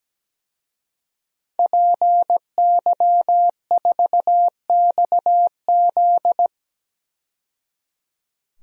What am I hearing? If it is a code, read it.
Morse code, heard as a keyed tone, about 17 wpm, 700 Hz. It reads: PY4XZ
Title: DXCC-100 Call Signs - 5 Characters - 17wpm